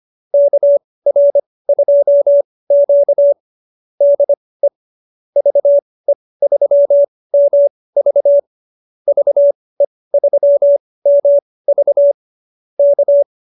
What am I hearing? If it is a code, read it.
Morse code, heard as a keyed tone, about 25 wpm, spaced as for 18 wpm, 590 Hz. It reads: KR2Q DE VE3MV VE3MV K